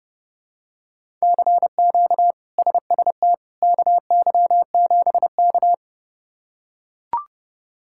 Morse code, heard as a keyed tone, about 30 wpm, 700 Hz. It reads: CQ SST KY7K
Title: K1USN SST QSO Practice - 30wpm